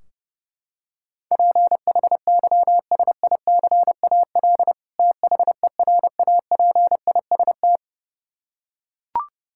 Morse code, heard as a keyed tone, about 30 wpm, 700 Hz. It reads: PHYSICAL THERAPIST
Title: Practice QSO - Exchange 3 - More Info - 30wpm